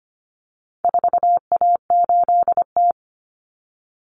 Morse code, heard as a keyed tone, about 25 wpm, 700 Hz. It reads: 4A8T
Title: Common Cut Numbers - 4-Digit 25wpm